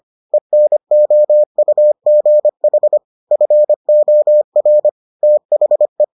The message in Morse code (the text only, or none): ENOUGH FOR THE